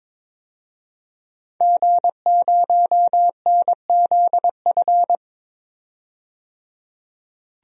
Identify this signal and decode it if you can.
Morse code, heard as a keyed tone, about 22 wpm, 700 Hz. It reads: G0NZF
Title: DXCC-100 Call Signs - 5 Characters - 22wpm